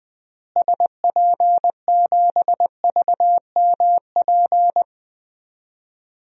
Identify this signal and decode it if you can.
Morse code, heard as a keyed tone, about 21 wpm, 700 Hz. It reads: SP7VMP
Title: DXCC-100 Call Signs - 6 Characters - Repeated 3x - 20wpm